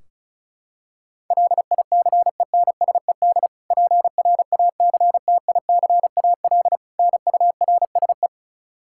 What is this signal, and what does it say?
Morse code, heard as a keyed tone, about 36 wpm, 700 Hz. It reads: LICENSED PRACTICAL NURSE
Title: Practice QSO - Exchange 3 - More Info - 35wpm